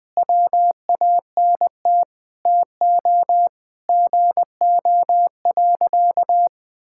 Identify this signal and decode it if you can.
Morse code, heard as a keyed tone, about 21 wpm, 700 Hz. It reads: WANT TO GO.
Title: Sentences with 4-Word Phrases - Spoken Beginning - 20wpm